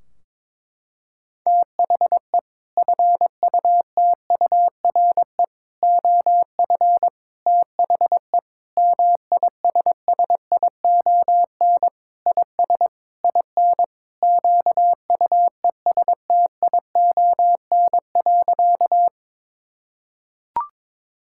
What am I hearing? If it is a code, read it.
Morse code, heard as a keyed tone, about 22 wpm, 700 Hz. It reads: THE FUTURE OF THE MISSION IS IN QUESTION.
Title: Sentences from Top 1600 Words 22wpm